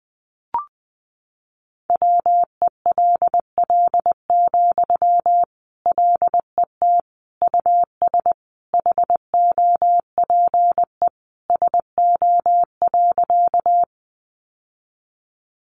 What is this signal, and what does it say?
Morse code, heard as a keyed tone, about 20 wpm, 700 Hz. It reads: WELL, LET US HOPE SO.
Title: Sentences from Top 1000 Words 20wpm